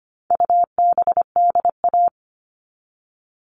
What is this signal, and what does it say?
Morse code, heard as a keyed tone, about 25 wpm, 700 Hz. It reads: UBDA
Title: All Cut Numbers - 4-Digit 25wpm